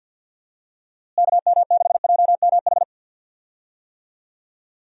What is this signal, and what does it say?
Morse code, heard as a keyed tone, about 50 wpm, 700 Hz. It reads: KM6JMH
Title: USA Call Signs - 6 Characters - Repeated 3x - 50wpm